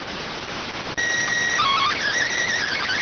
SSTV (slow-scan television), PD 290.